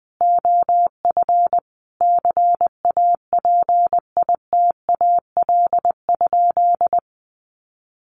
Morse code, heard as a keyed tone, about 20 wpm, 700 Hz. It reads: OF CAPITAL?
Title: Sentences from Top 1100 Words 20wpm